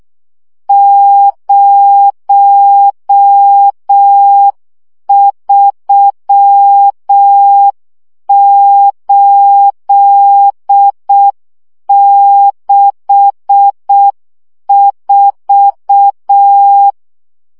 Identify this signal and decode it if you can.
Morse code, heard as a keyed tone, about 6 wpm, 800 Hz. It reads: 03864
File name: morze vizsga 2011-2 5